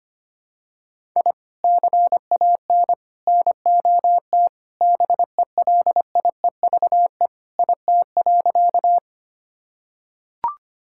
Morse code, heard as a keyed tone, about 25 wpm, 700 Hz. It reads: I CAN NOT BELIEVE IT.
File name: practice / 25wpm / Sentences with 4-Word Phrases - 25wpm